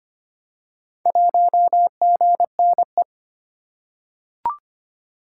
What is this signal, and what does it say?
Morse code, heard as a keyed tone, about 25 wpm, 700 Hz. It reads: 1GNE
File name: practice / 25wpm / Four Letter-Number Combinations - 25wpm